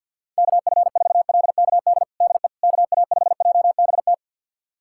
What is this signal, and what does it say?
Morse code, heard as a keyed tone, about 52 wpm, 700 Hz. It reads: KF4LKD DE KA5JBT
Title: USA Callsign DE Callsign - 6 Characters - 50wpm